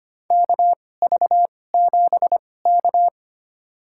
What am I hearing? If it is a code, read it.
Morse code, heard as a keyed tone, about 25 wpm, spaced as for 18 wpm, 700 Hz. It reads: KV7K
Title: USA Call Signs - 4 Characters - 2x Character Spacing - 25wpm